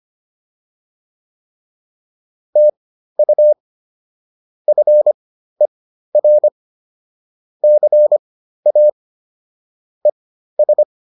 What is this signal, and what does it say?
Morse code, heard as a keyed tone, about 25 wpm, spaced as for 13 wpm, 600 Hz. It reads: TU FER CA ES